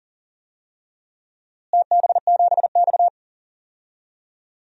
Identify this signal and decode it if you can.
Morse code, heard as a keyed tone, about 40 wpm, 700 Hz. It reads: TB7X
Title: DXCC-100 Call Signs - 4 Characters - Repeated 2x - 40wpm